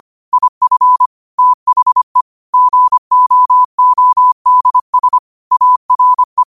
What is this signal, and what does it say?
Morse code, heard as a keyed tone, about 25 wpm, 1000 Hz. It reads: IF THE GOODS ARE
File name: qotdfiles / 25wpm